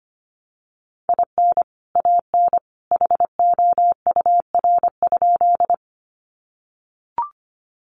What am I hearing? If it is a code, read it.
Morse code, heard as a keyed tone, about 25 wpm, 700 Hz. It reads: IN AN HOUR?
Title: Sentences from Top 400 Words 25wpm